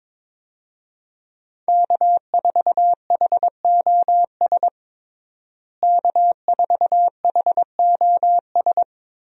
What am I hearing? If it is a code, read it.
Morse code, heard as a keyed tone, about 21 wpm, 700 Hz. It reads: K4HOS K4HOS
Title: USA Call Signs - 5 Characters - Repeated 3x - 22wpm